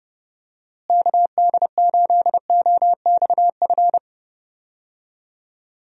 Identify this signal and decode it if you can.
Morse code, heard as a keyed tone, about 30 wpm, 700 Hz. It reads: KD8OXF
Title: USA Call Signs - 6 Characters - Repeated 3x - 30wpm